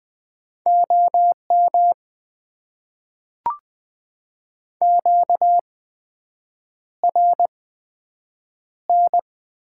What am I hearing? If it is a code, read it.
Morse code, heard as a keyed tone, about 20 wpm, spaced as for 7 wpm, 700 Hz. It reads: OM QRN
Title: Top 90 QSO Elements - 8x Character Spacing - 20wpm